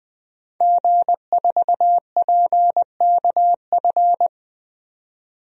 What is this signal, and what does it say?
Morse code, heard as a keyed tone, about 20 wpm, 700 Hz. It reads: G4PKF
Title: DXCC-100 Call Signs - 5 Characters - 20wpm